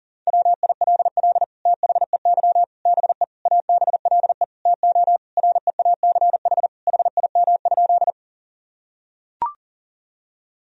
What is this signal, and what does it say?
Morse code, heard as a keyed tone, about 37 wpm, 700 Hz. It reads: WILL THEY BE ABLE TO REACH HIM?
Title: Sentences from Top 1000 Words - Review Part 2 - 40wpm